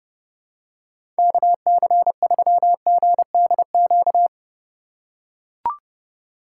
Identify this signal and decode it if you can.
Morse code, heard as a keyed tone, about 30 wpm, 700 Hz. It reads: KC3GDQ